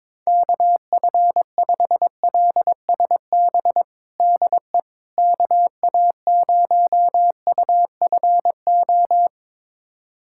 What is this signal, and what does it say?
Morse code, heard as a keyed tone, about 22 wpm, 700 Hz. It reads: KF5LSB DE KA0UFO
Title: USA Callsign DE Callsign 22wpm